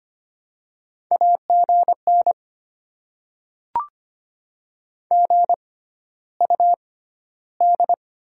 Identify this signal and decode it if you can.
Morse code, heard as a keyed tone, about 25 wpm, spaced as for 13 wpm, 700 Hz. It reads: AGN GUD
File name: Top 20 QSO Elements - 6x Character Spacing - 25wpm